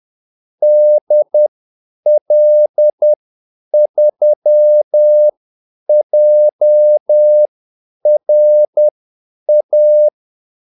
Morse code, heard as a keyed tone, about 10 wpm, spaced as for 8 wpm, 600 Hz. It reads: DL3JRA
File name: Callsigns-10-8wpm